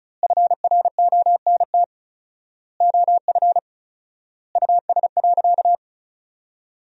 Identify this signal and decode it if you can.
Morse code, heard as a keyed tone, about 34 wpm, 700 Hz. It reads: FRONT OF US.
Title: 4x Word-Spacing - Sentences from Top 500 Words - 35wpm